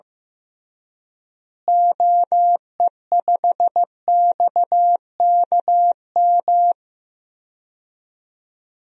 Morse code, heard as a keyed tone, about 15 wpm, 700 Hz. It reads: OE5XKM
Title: DXCC-100 Call Signs - 6 Characters - Repeated 3x - 15wpm